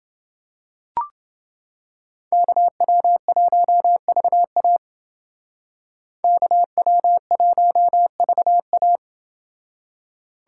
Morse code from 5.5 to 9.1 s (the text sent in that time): KW1VA